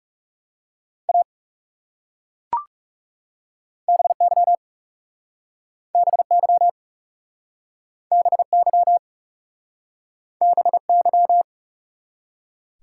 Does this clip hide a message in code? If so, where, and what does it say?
Yes: Morse code from 1.1 to 11.4 s, heard as a keyed tone, about 40 wpm, 700 Hz. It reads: A BY BY BY BY